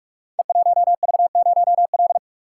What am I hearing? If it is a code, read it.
Morse code, heard as a keyed tone, about 44 wpm, 700 Hz. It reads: E1V0L
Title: Code Groups - 5 Letter:Numbers - 45wpm